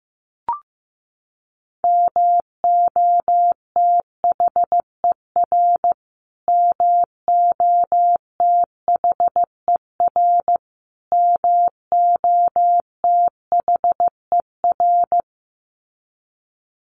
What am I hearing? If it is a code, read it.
Morse code, heard as a keyed tone, about 15 wpm, 700 Hz. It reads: MOTHER MOTHER MOTHER
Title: Top 300 Words Repeated 15wpm